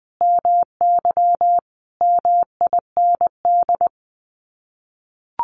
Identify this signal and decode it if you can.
Morse code, heard as a keyed tone, about 20 wpm, 700 Hz. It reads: MY MIND
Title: Sets of 5 Words Encore2 20wpm